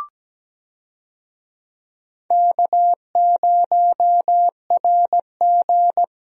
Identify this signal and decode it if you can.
Morse code, heard as a keyed tone, about 17 wpm, 700 Hz. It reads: K0RG
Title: USA Call Signs - 4 Characters - Repeated 2x - 17wpm